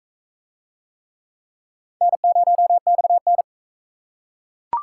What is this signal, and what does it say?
Morse code, heard as a keyed tone, about 40 wpm, 700 Hz. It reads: N0XN